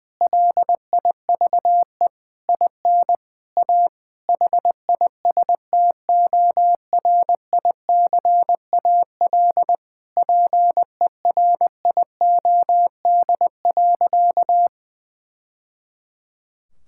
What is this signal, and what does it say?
Morse code, heard as a keyed tone, about 20 wpm, 700 Hz. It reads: LIVE IN A HISTORICAL PERIOD.
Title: Sentences from Top 1000 Words - Review Part 5 - 20wpm